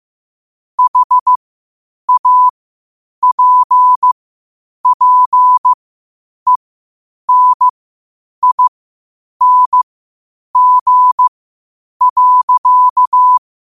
Morse code, heard as a keyed tone, about 15 wpm, spaced as for 8 wpm, 1000 Hz. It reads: HAPPENING.
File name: qotdfiles / 8wpm